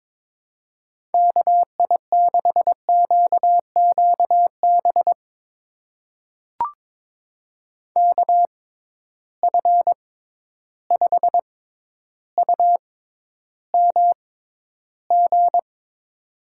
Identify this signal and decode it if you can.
Morse code, heard as a keyed tone, about 22 wpm, spaced as for 11 wpm, 700 Hz. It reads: KI6QQB KF5UMG
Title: USA Call Signs - 6 Characters - 6x Character Spacing - 22wpm